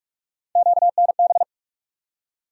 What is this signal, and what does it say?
Morse code, heard as a keyed tone, about 42 wpm, 700 Hz. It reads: QNB
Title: USA Call Sign Postfixes - 3 Letters 45wpm